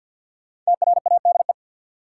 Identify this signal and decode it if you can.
Morse code, heard as a keyed tone, about 53 wpm, 700 Hz. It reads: TRADE
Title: Sets of 5 Words Encore - New Word Drill - 50wpm